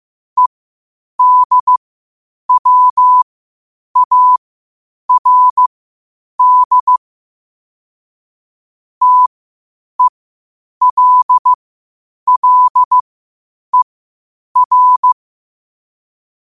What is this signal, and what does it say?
Morse code, heard as a keyed tone, about 15 wpm, spaced as for 8 wpm, 1000 Hz. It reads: EDWARD TELLER
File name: qotdfiles / 8wpm